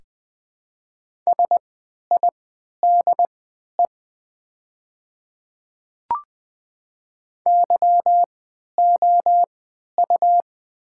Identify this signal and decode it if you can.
Morse code, heard as a keyed tone, about 20 wpm, spaced as for 11 wpm, 700 Hz. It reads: SIDE YOU
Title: 3x Character Spacing - Top 100 Words - 20wpm